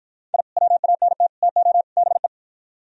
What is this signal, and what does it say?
Morse code, heard as a keyed tone, about 51 wpm, 700 Hz. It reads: I WANT TO BE